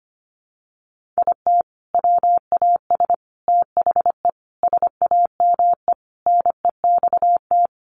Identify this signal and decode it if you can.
Morse code, heard as a keyed tone, about 25 wpm, 700 Hz. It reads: IT WAS THE SAME NEXT
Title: Sentences from Top 500 Words 25wpm